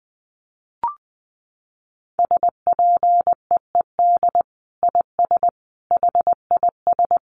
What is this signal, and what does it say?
Morse code, heard as a keyed tone, about 20 wpm, 700 Hz. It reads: SPEED IS HIS
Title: Sentences from Top 1300 Words 20wpm